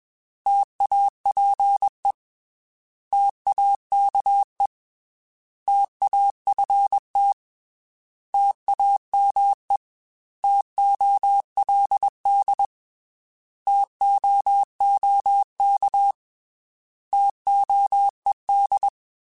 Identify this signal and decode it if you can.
Morse code, heard as a keyed tone, about 21 wpm, 800 Hz. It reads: TAPE TAKE TAFT TAME TOLD TOOK TOED
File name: T Prefixes / T mp3s